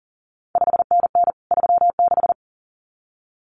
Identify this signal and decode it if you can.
Morse code, heard as a keyed tone, about 38 wpm, 700 Hz. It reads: 5NN 36
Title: CQ WW DX Contest Exchange Encore - 40wpm